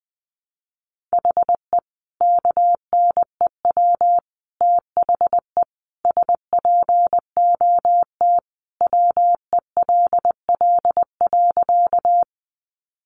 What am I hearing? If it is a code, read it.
Morse code, heard as a keyed tone, about 20 wpm, 700 Hz. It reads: HE KNEW THE SPOT WELL.
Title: Sentences from Top 1100 Words 20wpm